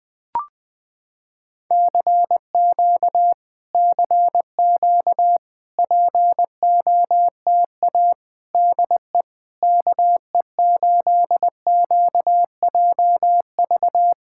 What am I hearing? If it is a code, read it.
Morse code, heard as a keyed tone, about 20 wpm, 700 Hz. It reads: CQ CQ POTA DE KE8QJV
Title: POTA QSO Practice - 20wpm